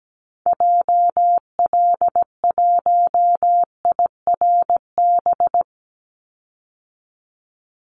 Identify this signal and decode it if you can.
Morse code, heard as a keyed tone, about 17 wpm, 700 Hz. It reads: JL1IRB
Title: Call Signs Repeated 3x - 17wpm